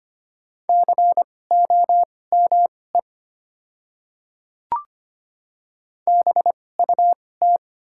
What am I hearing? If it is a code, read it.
Morse code, heard as a keyed tone, about 25 wpm, spaced as for 18 wpm, 700 Hz. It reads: COME BUT
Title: 2x Character Spacing - Top 100 Words - 25wpm